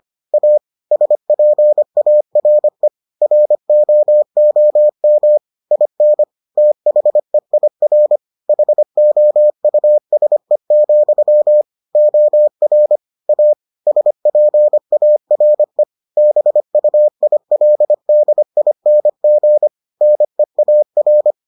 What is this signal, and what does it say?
Morse code, heard as a keyed tone, about 25 wpm, 600 Hz. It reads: A SPARE ROOM IN THEIR HOUSE, OR A SPARE BUILDING NEAR